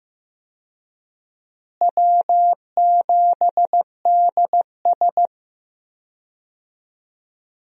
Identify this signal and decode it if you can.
Morse code, heard as a keyed tone, about 14 wpm, 700 Hz. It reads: W7DS
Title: USA Call Signs - 4 Characters - Repeated 3x - 15wpm